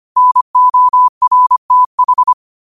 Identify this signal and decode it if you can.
Morse code, heard as a keyed tone, about 25 wpm, 1000 Hz. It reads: NORTH
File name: qotdfiles / 25wpm